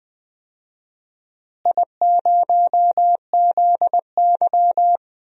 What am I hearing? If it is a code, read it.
Morse code, heard as a keyed tone, about 20 wpm, 700 Hz. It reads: I0ZY
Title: DXCC-100 Call Signs - 4 Characters - Repeated 3x - 20wpm